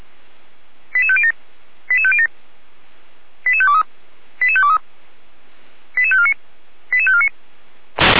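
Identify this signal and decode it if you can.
Five-tone selective call (ZVEI), heard as quick runs of five tones, tones of about 70 ms. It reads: 88507 88507 88523 88523 88549 88549
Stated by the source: Received on an SDR, ZVEI five-tone code 88427.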